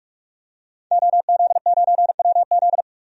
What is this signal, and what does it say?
Morse code, heard as a keyed tone, about 46 wpm, 700 Hz. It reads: OZ9WZ
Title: DXCC-100 Call Signs - 5 Characters - 45wpm